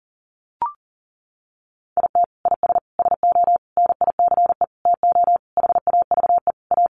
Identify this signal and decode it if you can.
Morse code, heard as a keyed tone, about 40 wpm, 700 Hz. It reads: IT IS SO NICE TO HAVE A
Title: Sentences from Top 1100 Words 40wpm